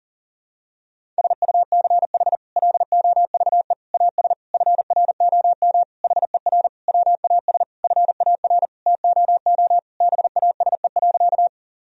Morse code, heard as a keyed tone, about 40 wpm, 700 Hz. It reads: SUCH LOVE AS FROM HER WAS FAR TOO BASE.
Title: Sentences from Top 1000 Words - Review Part 1 - 40wpm